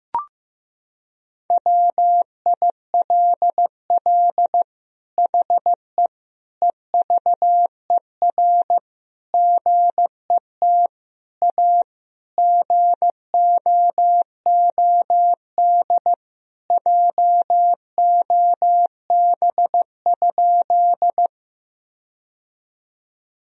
Morse code, heard as a keyed tone, about 15 wpm, 700 Hz. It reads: WILL HE EVER GET A GOOD JOB?
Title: Sentences from Top 1000 Words - Review Part 3 - 15wpm